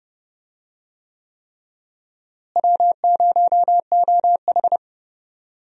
Morse code, heard as a keyed tone, about 30 wpm, 700 Hz. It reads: W0OH
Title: USA Call Signs - 4 Characters - Repeated 3x - 30wpm